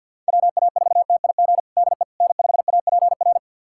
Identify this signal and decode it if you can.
Morse code, heard as a keyed tone, about 49 wpm, 700 Hz. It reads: WA4TIG DE N5APR
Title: USA Callsign DE Callsign 50wpm